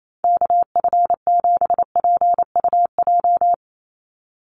Morse code, heard as a keyed tone, about 28 wpm, 700 Hz. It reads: KF7PUJ
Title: USA Call Signs 28wpm